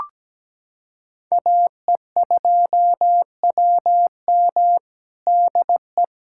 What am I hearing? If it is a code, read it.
Morse code, heard as a keyed tone, about 17 wpm, 700 Hz. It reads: AE2WM DE
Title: USA Callsign DE Callsign - 5 Characters - 17wpm